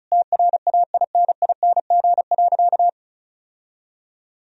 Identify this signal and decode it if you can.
Morse code, heard as a keyed tone, about 36 wpm, 700 Hz. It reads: TRAINING.
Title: Sentences from Top 1000 Words - Review Part 4 - 35wpm